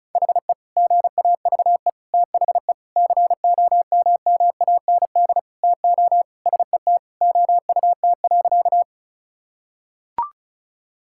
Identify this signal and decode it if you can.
Morse code, heard as a keyed tone, about 36 wpm, 700 Hz. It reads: HE GAVE THE COMMAND TO SET OUT.
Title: Sentences from Top 1600 Words 35wpm